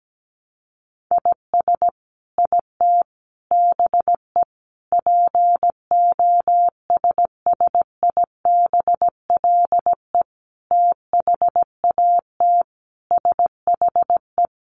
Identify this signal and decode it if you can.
Morse code, heard as a keyed tone, about 17 wpm, 700 Hz. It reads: IS IT BE POSSIBLE THAT SHE
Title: Sentences from Top 1200 Words 17wpm